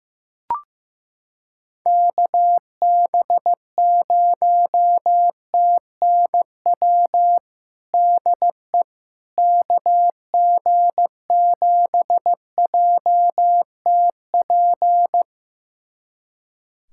Morse code, heard as a keyed tone, about 16 wpm, 700 Hz. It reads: KB0TNW DE KG7JTP
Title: USA Callsign DE Callsign - 6 Characters - 15wpm